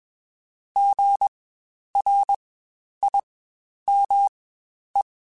Morse code, heard as a keyed tone, about 21 wpm, spaced as for 10 wpm, 800 Hz. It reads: GRIME
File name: Spaced Alphas